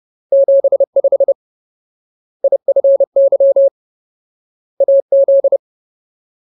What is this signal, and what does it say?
Morse code, heard as a keyed tone, about 30 wpm, 550 Hz. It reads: 75 IFY AZ
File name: Ultimate Word List / 30WPM Ultimate